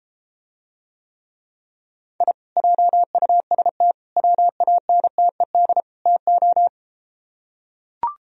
Morse code, heard as a keyed tone, about 33 wpm, 700 Hz. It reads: I JUST WANTED TO